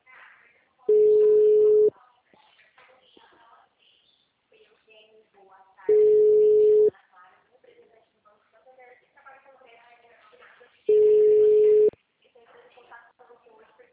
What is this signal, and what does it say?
Ringback tone (Europe, 425 Hz), with speech.